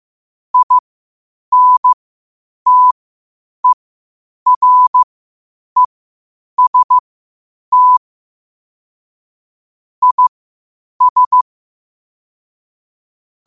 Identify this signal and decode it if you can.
Morse code, heard as a keyed tone, about 15 wpm, spaced as for 8 wpm, 1000 Hz. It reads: INTEREST IS
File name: qotdfiles / 8wpm